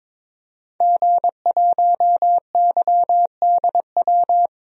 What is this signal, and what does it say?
Morse code, heard as a keyed tone, about 22 wpm, 700 Hz. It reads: G1YDW
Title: DXCC-100 Call Signs - 5 Characters - Repeated 2x - 22wpm